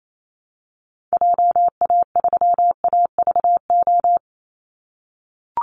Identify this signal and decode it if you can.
Morse code, heard as a keyed tone, about 28 wpm, 700 Hz. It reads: JA3AVO
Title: Call Signs - 28wpm